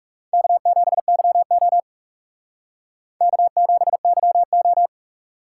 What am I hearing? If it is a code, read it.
Morse code, heard as a keyed tone, about 37 wpm, 700 Hz. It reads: K7YO K7YO
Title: Speed Racing - USA 1x2 Call Signs 30wpm